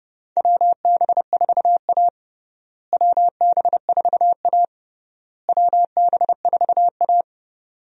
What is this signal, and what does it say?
Morse code, heard as a keyed tone, about 30 wpm, 700 Hz. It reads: WB4A WB4A WB4A
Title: USA Call Signs - 4 Characters - Repeated 3x - 30wpm